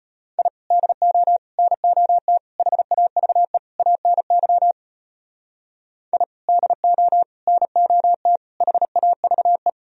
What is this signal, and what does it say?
Morse code, heard as a keyed tone, about 33 wpm, 700 Hz. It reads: I DO NOT HAVE ANY I DO NOT HAVE